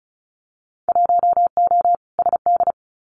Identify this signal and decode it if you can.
Morse code, heard as a keyed tone, about 34 wpm, 700 Hz. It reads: 1O SD
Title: Winter Field Day Exchanges Encore 35wpm